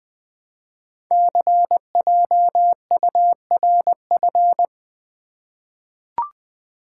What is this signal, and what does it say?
Morse code, heard as a keyed tone, about 20 wpm, 700 Hz. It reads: CJURF
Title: Code Groups - 5 Letters - 20wpm